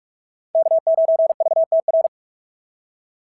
Morse code, heard as a keyed tone, about 44 wpm, 690 Hz. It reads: K9VTR
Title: SOTA QSO Practice 45wpm